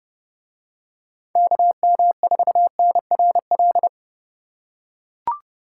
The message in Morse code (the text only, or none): KM4NRL